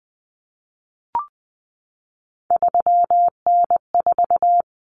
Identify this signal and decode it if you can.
Morse code, heard as a keyed tone, about 20 wpm, 700 Hz. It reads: 3N4
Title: Common Cut Numbers - 3-Digit 20wpm